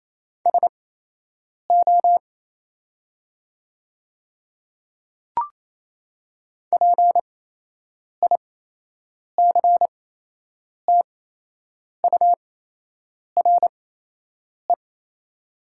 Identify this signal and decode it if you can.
Morse code, heard as a keyed tone, about 28 wpm, spaced as for 8 wpm, 700 Hz. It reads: SO PICTURE
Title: 8x Character Spacing - Top 500 Words - 28wpm